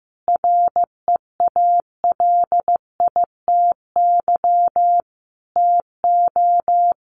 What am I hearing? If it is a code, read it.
Morse code, heard as a keyed tone, about 15 wpm, 700 Hz. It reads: REALITY TO
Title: Sentences from Top 1200 Words 15wpm